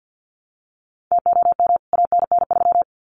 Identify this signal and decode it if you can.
Morse code, heard as a keyed tone, about 49 wpm, 700 Hz. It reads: TOM ANN3
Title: CWT Exchange 50wpm